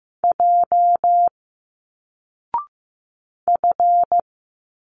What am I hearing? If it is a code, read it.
Morse code, heard as a keyed tone, about 15 wpm, 700 Hz. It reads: J F
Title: Single Letters 15wpm